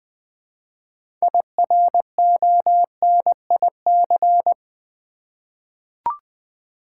Morse code, heard as a keyed tone, about 20 wpm, 700 Hz. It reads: IRONIC
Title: 6 Letter Words - First 3 Letters Given - 20wpm